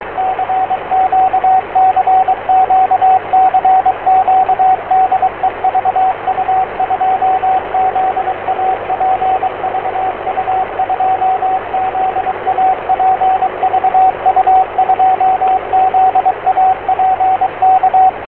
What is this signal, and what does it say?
Morse code, heard as a keyed tone, about 23 wpm, 750 Hz, with radio noise. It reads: CQ CQ CQ DE VU2ZAP VU2ZAP VU2ZAP K